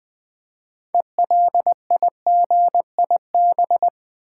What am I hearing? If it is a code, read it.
Morse code, heard as a keyed tone, about 20 wpm, 700 Hz. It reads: ELIGIB
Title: Build A Word - 3 Letters At A Time - 8-Letter Words - 20wpm